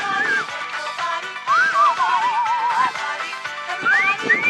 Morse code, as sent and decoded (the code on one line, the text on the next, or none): none